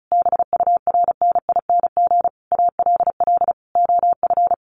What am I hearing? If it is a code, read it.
Morse code, heard as a keyed tone, about 34 wpm, 700 Hz. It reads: BURNING ALL OF
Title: A Sound Of Thunder 35wpm